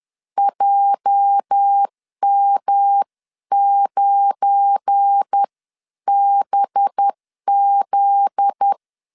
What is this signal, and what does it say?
Morse code, heard as a keyed tone, about 10 wpm, spaced as for 9 wpm, 800 Hz. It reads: JM9BZ
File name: LEZIONI / 9 28 10wpm 2842